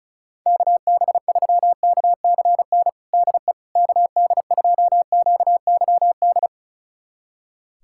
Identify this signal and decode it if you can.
Morse code, heard as a keyed tone, about 34 wpm, 690 Hz. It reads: KB3KCN DE KD2QYD
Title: USA Callsign DE Callsign - 6 Characters - 35wpm